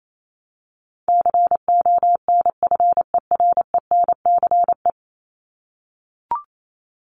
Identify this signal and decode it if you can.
Morse code, heard as a keyed tone, about 28 wpm, 700 Hz. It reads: CONFERENCE
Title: Top 1751-2000 Words 28wpm